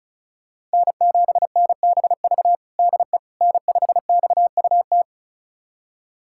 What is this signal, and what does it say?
Morse code, heard as a keyed tone, about 36 wpm, 700 Hz. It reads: N7NBV DE N5XUT
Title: USA Callsign DE Callsign - 5 Characters - 35wpm